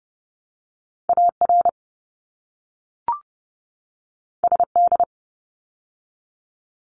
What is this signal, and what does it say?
Morse code, heard as a keyed tone, about 30 wpm, 700 Hz. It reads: AR SD
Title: ARRL Sections 30wpm